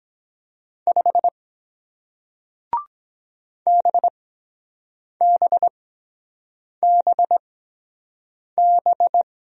Morse code, heard as a keyed tone, about 23 wpm, spaced as for 7 wpm, 700 Hz. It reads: 5 BBBB